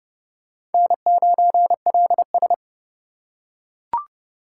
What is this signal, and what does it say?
Morse code, heard as a keyed tone, about 30 wpm, 700 Hz. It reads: N9LS
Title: USA Call Signs - 4 Characters - 4x Character Spacing - 30wpm